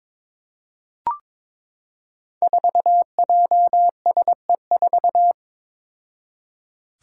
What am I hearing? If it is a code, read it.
Morse code, heard as a keyed tone, about 22 wpm, 700 Hz. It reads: 4JSE4
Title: Five Challenging Letter-Number Combinations 22wpm